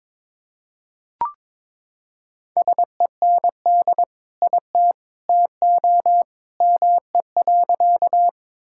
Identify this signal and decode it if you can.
Morse code, heard as a keyed tone, about 21 wpm, 700 Hz. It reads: SEND IT TO ME.
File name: practice / 22wpm / Sentences with 2-Word Phrases - 22wpm